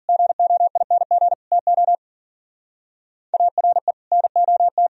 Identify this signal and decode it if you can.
Morse code, heard as a keyed tone, about 50 wpm, 700 Hz. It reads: GOING TO ARE NOT